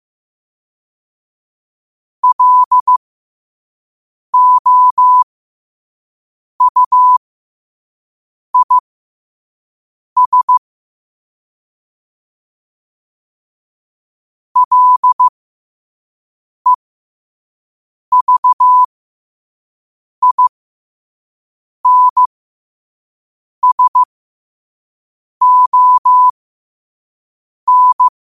Morse code, heard as a keyed tone, about 15 wpm, spaced as for 5 wpm, 1000 Hz. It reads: LOUIS LEVINSON
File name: qotdfiles / 5wpm